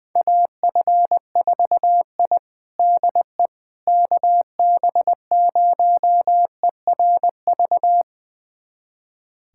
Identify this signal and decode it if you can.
Morse code, heard as a keyed tone, about 21 wpm, 700 Hz. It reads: AF4I DE KB0ERV
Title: USA Callsign DE Callsign 20wpm